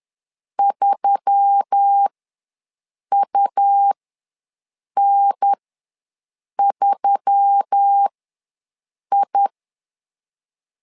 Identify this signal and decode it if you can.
Morse code, heard as a keyed tone, about 10 wpm, spaced as for 6 wpm, 800 Hz. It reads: 3UN3I
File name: LEZIONI / 4 Q1S3 ws 10wpm 2437